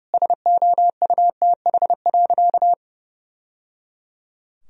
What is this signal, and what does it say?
Morse code, heard as a keyed tone, about 30 wpm, 700 Hz. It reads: SOUTH.
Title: Sentences from Top 1000 Words - Review Part 2 - 30wpm